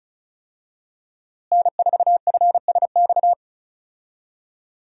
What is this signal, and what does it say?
Morse code, heard as a keyed tone, about 36 wpm, 700 Hz. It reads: N4FSX
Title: USA Call Signs - 5 Characters - Repeated 3x - 35wpm